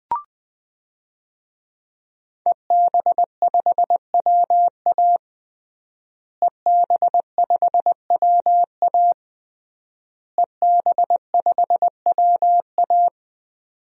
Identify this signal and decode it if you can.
Morse code, heard as a keyed tone, about 21 wpm, 700 Hz. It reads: EB5WA EB5WA EB5WA
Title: DXCC-100 Call Signs - 5 Characters - Repeated 3x - 20wpm